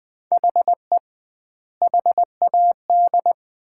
Morse code, heard as a keyed tone, about 20 wpm, 700 Hz. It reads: HE HAD
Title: 2x Word-Spacing - Sets of 3 Words - 20wpm